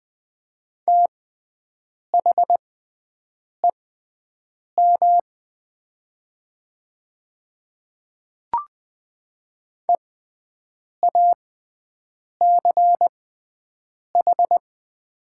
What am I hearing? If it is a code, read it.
Morse code, heard as a keyed tone, about 20 wpm, spaced as for 7 wpm, 700 Hz. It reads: THEM EACH
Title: 6x Character Spacing - Top 100 Words - 20wpm